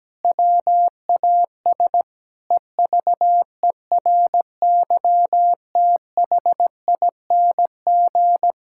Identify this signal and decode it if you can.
Morse code, heard as a keyed tone, about 17 wpm, 700 Hz. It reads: WAS EVERYTHING
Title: Sentences from Top 1000 Words - Review Part 6 - 17wpm